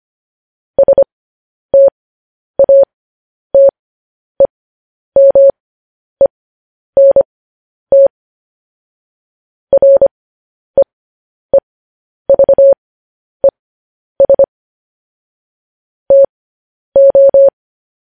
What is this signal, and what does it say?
Morse code, heard as a keyed tone, about 25 wpm, spaced as for 10 wpm, 550 Hz. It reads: STATEMENT REEVES TO
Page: news-headlines-in-morse-code-at-10-wpm